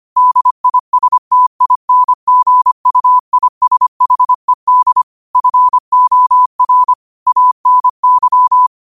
Morse code, heard as a keyed tone, about 25 wpm, 1000 Hz. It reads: DISTINGUISHED FOR ANY
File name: qotdfiles / 25wpm